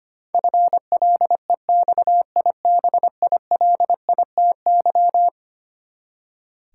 Morse code, heard as a keyed tone, about 25 wpm, 700 Hz. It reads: FLEXIBILITY
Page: morse-code-ninja-practice-25wpm